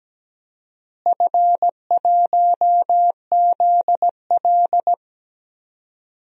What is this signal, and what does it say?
Morse code, heard as a keyed tone, about 17 wpm, 700 Hz. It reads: F1ZL
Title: DXCC-100 Call Signs - 4 Characters - Repeated 2x - 17wpm